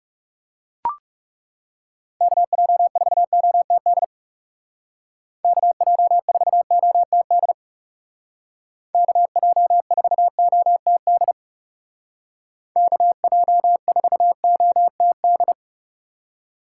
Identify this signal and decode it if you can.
Morse code, heard as a keyed tone, about 37 wpm, 700 Hz. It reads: KJ4OTD KJ4OTD KJ4OTD KJ4OTD
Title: Speed Racing - USA Call Signs 30wpm